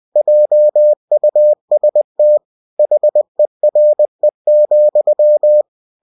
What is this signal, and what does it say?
Morse code, heard as a keyed tone, about 20 wpm, 600 Hz. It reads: JUST HERE,
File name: cw-books / winnie-the-pooh / winnie-20wpm-page-0022